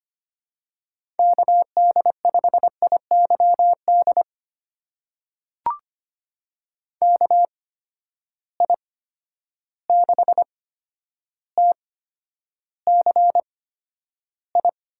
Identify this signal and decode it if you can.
Morse code, heard as a keyed tone, about 25 wpm, spaced as for 11 wpm, 700 Hz. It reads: KD5IYD KI6TCI
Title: USA Call Signs - 6 Characters - 8x Character Spacing - 25wpm